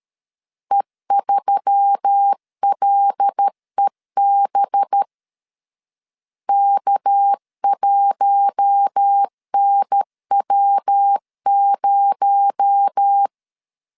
Morse code, heard as a keyed tone, about 13 wpm, 800 Hz. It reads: E3LEB K1NW0
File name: CW / DET / 1 6 1500C S B 12wpm 3431